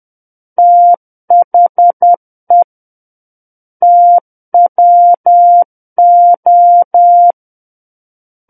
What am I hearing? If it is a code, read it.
Morse code, heard as a keyed tone, about 10 wpm, 700 Hz. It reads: THE TWO
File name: Book One / 01-Chapter 1 - 10 WPM